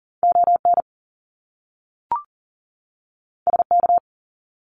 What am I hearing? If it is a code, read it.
Morse code, heard as a keyed tone, about 40 wpm, 700 Hz. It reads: ON SK
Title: Canadian Provinces and Territories 40wpm